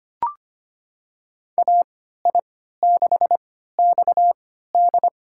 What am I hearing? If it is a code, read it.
Morse code, heard as a keyed tone, about 25 wpm, spaced as for 14 wpm, 700 Hz. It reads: AI6XD
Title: USA Call Signs - 5 Characters - 3x Character Spacing - 25wpm